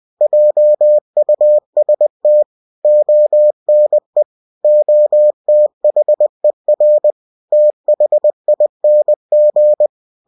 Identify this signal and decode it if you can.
Morse code, heard as a keyed tone, about 20 wpm, 600 Hz. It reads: JUST ONE OTHER THING
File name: cw-books / winnie-the-pooh / winnie-20wpm-page-0035